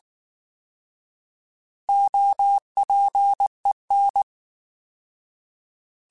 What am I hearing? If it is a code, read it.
Morse code, heard as a keyed tone, about 19 wpm, 800 Hz. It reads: OPEN
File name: Alphabetical / Alphabetical mp3s / Os @18